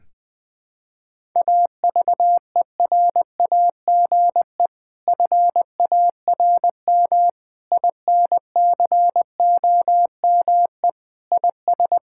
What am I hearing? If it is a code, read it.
Morse code, heard as a keyed tone, about 20 wpm, 700 Hz. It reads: AVERAGE FARM INCOME IS
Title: Sentences from Top 1000 Words - Review Part 6 - 20wpm